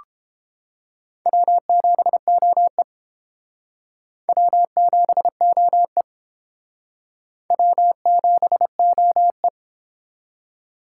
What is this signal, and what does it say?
Morse code, heard as a keyed tone, about 29 wpm, 700 Hz. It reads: W7OE W7OE W7OE